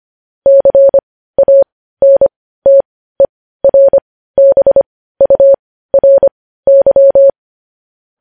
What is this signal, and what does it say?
Morse code, heard as a keyed tone, about 25 wpm, spaced as for 15 wpm, 550 Hz. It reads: CANTERBURY
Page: news-headlines-in-morse-code-at-15-wpm